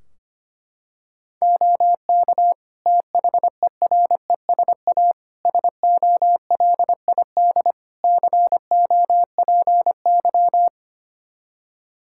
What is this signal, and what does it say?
Morse code, heard as a keyed tone, about 25 wpm, 700 Hz. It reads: OK THERESA SOLID COPY
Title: Practice QSO - Exchange 3 - More Info - 25wpm